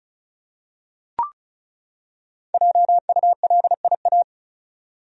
Morse code, heard as a keyed tone, about 36 wpm, 700 Hz. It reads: JULIA
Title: USA Common Names - 35wpm